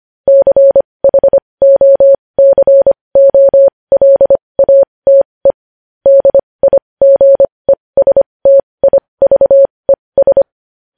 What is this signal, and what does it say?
Morse code, heard as a keyed tone, about 25 wpm, spaced as for 20 wpm, 550 Hz. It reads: CHOCOLATE DIGESTIVES
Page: News Headlines in Morse Code at 20 WPM